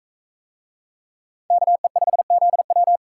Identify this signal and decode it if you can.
Morse code, heard as a keyed tone, about 39 wpm, 700 Hz. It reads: KE5ZW